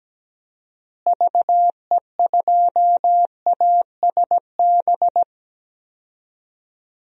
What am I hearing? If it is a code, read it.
Morse code, heard as a keyed tone, about 17 wpm, 700 Hz. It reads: VE2ASB
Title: DXCC-100 Call Signs - 6 Characters - Repeated 3x - 17wpm